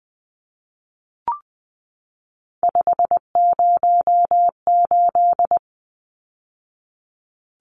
Morse code, heard as a keyed tone, about 20 wpm, 700 Hz. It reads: 508
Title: Numbers - 3-Digit 20wpm